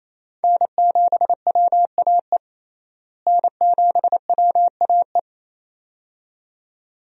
Morse code, heard as a keyed tone, about 28 wpm, 700 Hz. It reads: N7WAE N7WAE
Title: USA Call Signs - 5 Characters - Repeated 2x - 28wpm